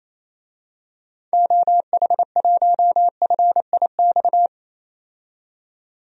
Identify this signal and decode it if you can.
Morse code, heard as a keyed tone, about 28 wpm, 700 Hz. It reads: OH1FIX
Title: DXCC-100 Call Signs - 6 Characters - 28wpm